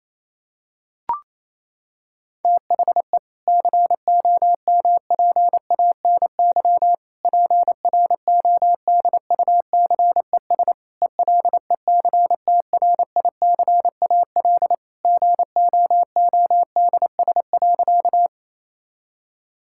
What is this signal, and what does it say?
Morse code, heard as a keyed tone, about 28 wpm, 700 Hz. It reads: THE COMPANY PRODUCES ELECTRICAL GOODS.
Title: Sentences from Top 1000 Words - Review Part 6 - 28wpm